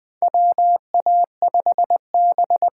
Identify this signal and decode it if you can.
Morse code, heard as a keyed tone, about 20 wpm, 700 Hz. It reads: WA5B
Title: USA Call Signs - 6 Characters - 8x Character Spacing - 20wpm